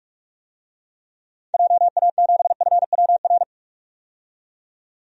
Morse code, heard as a keyed tone, about 43 wpm, 700 Hz. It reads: JA7FWR
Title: DXCC-100 Call Signs - 6 Characters - 45wpm